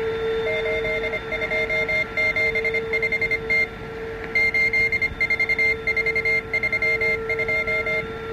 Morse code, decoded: 8275T 84432